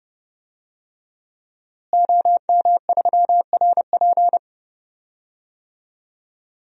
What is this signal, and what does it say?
Morse code, heard as a keyed tone, about 30 wpm, 700 Hz. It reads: OM3RP
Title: DXCC-100 Call Signs - 5 Characters - Repeated 3x - 30wpm